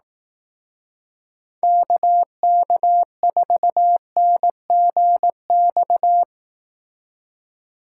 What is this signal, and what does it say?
Morse code, heard as a keyed tone, about 18 wpm, 700 Hz. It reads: KK4NGX
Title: Speed Racing - USA 2x3 Call Signs 15wpm